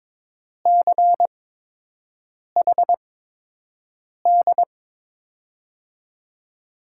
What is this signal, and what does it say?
Morse code, heard as a keyed tone, about 22 wpm, spaced as for 6 wpm, 700 Hz. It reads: CHD
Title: USA Call Signs - 5 Characters - 8x Character Spacing - 22wpm